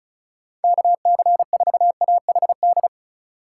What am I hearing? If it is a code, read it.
Morse code, heard as a keyed tone, about 34 wpm, 700 Hz. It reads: KC4AHD
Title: USA Call Signs - 6 Characters - 2x Character Spacing - 35wpm